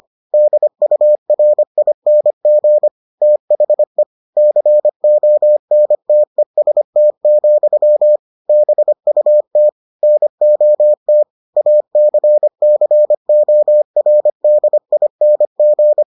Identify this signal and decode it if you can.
Morse code, heard as a keyed tone, about 25 wpm, 600 Hz. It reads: DURING THE CONTEST, BUT NOT ACCORDING